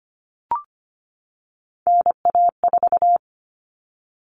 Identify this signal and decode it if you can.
Morse code, heard as a keyed tone, about 25 wpm, 700 Hz. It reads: NA4
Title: All Cut Numbers - 3-Digit 25wpm